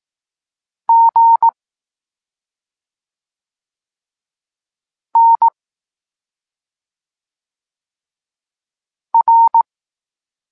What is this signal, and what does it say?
Morse code, heard as a keyed tone, about 18 wpm, 920 Hz. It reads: G N R
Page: Tekst 23.3 – Ciąg dalszy doskonalenia odbioru poznanych liter